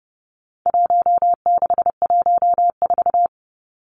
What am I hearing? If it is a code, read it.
Morse code, heard as a keyed tone, about 30 wpm, 700 Hz. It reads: 1614
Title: Numbers - Contest Style - 4-Digit 30wpm